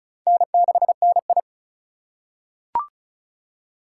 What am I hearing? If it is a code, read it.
Morse code, heard as a keyed tone, about 34 wpm, 700 Hz. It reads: N6NI
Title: USA Call Signs - 4 Characters - 4x Character Spacing - 35wpm